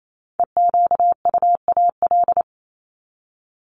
Morse code, heard as a keyed tone, about 28 wpm, 700 Hz. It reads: EQUAL
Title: Top 1251-1500 Words 28wpm